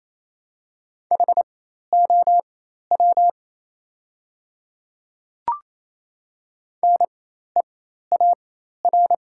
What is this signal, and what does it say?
Morse code, heard as a keyed tone, about 28 wpm, spaced as for 13 wpm, 700 Hz. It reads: HOW NEAR
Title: 4x Character Spacing - Top 200 Words - 28wpm